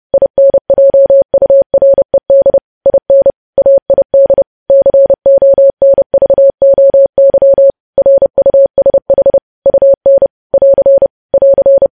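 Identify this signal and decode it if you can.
Morse code, heard as a keyed tone, about 30 wpm, 550 Hz. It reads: INJURED IN AID CONVOY RUSH UN <AR> <AR>
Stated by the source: Morse code practice of news headlines at 30 wpm, preceded by vvv.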